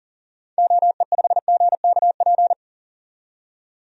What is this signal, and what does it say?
Morse code, heard as a keyed tone, about 40 wpm, 700 Hz. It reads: OE5GKP
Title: DXCC-100 Call Signs - 6 Characters - 40wpm